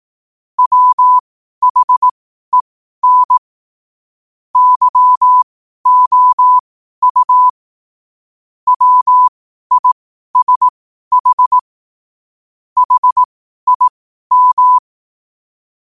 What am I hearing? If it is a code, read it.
Morse code, heard as a keyed tone, about 18 wpm, spaced as for 12 wpm, 1000 Hz. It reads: WHEN YOU WISH HIM
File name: qotdfiles / 12wpm